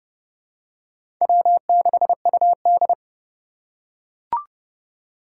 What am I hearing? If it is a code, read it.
Morse code, heard as a keyed tone, about 30 wpm, 700 Hz. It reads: W6UD
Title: USA Call Signs - 4 Characters - 3x Character Spacing - 30wpm